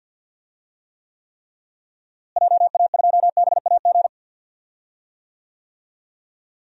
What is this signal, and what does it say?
Morse code, heard as a keyed tone, about 52 wpm, 700 Hz. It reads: JA2BAG
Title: DXCC-100 Call Signs - 6 Characters - Repeated 3x - 50wpm